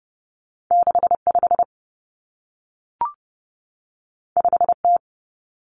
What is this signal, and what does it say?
Morse code, heard as a keyed tone, about 30 wpm, 700 Hz. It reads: 65 5T
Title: Common Cut Numbers - 2-Digit 30wpm